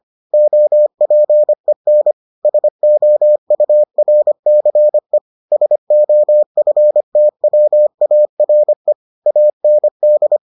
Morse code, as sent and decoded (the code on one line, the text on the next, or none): --- .--. -. / ... --- ..- .-. -.-. / ... --- ..-. - .-- .- .-. / .- -. -..
OPEN SOURCE SOFTWARE AND